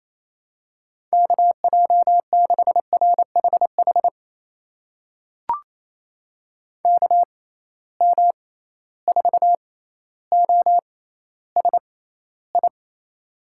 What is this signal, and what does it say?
Morse code, heard as a keyed tone, about 28 wpm, spaced as for 14 wpm, 700 Hz. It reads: KJ6RHH KM4OSI
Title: USA Call Signs - 6 Characters - 6x Character Spacing - 28wpm